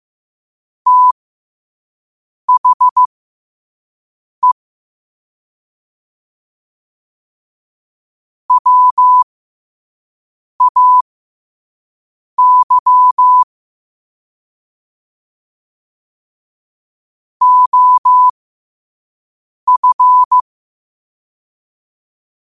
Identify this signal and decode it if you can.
Morse code, heard as a keyed tone, about 15 wpm, spaced as for 5 wpm, 1000 Hz. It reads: THE WAY OF
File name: qotdfiles / 5wpm